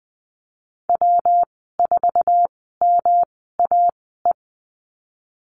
Morse code, heard as a keyed tone, about 20 wpm, spaced as for 15 wpm, 700 Hz. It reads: W4MAE
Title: 128 Lesson 21g - Number 6 - Call Signs 20wpm